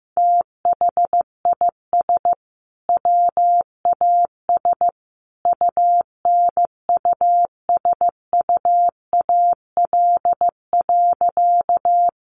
Morse code, heard as a keyed tone, about 15 wpm, 700 Hz. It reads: THIS WAS UNUSUAL.
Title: Sentences from Top 1400 Words 15wpm